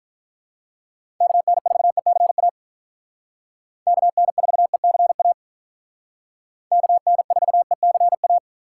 Morse code, heard as a keyed tone, about 45 wpm, 700 Hz. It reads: KN4ECA KN4ECA KN4ECA